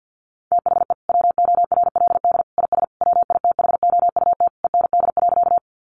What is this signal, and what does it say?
Morse code, heard as a keyed tone, about 50 wpm, 700 Hz. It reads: THE WORLD IS WITHOUT END.
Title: Sentences from Top 600 Words 50wpm